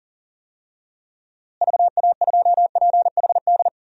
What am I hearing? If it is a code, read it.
Morse code, heard as a keyed tone, about 40 wpm, 700 Hz. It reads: VA2PHD
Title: DXCC-100 Call Signs - 6 Characters - Repeated 2x - 40wpm